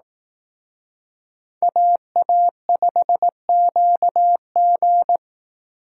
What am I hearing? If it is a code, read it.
Morse code, heard as a keyed tone, about 18 wpm, 700 Hz. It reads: AA5QG
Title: Speed Racing - USA Call Signs 15wpm